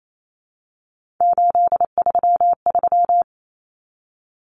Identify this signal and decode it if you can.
Morse code, heard as a keyed tone, about 28 wpm, 700 Hz. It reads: 833
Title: Numbers - 3-Digit 28wpm